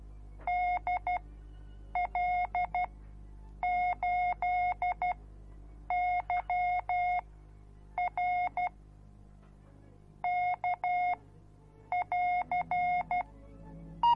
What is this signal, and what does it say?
Morse code, heard as a keyed tone, about 12 wpm, spaced as for 7 wpm, 720 Hz. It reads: DL8YR K<AR>